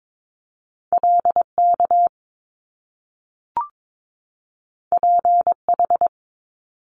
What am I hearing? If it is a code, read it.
Morse code, heard as a keyed tone, about 22 wpm, 700 Hz. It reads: LK PH
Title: Top Two Letter Combinations in Words 22wpm